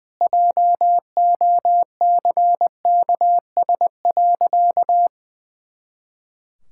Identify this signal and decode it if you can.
Morse code, heard as a keyed tone, about 20 wpm, 700 Hz. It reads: JOCKS.
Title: Pangram Challenge v2 - 20wpm